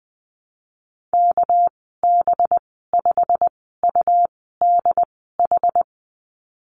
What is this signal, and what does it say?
Morse code, heard as a keyed tone, about 20 wpm, spaced as for 15 wpm, 700 Hz. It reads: KB5UDH
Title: 207 Lesson 33g - Letter j - Call Signs 20wpm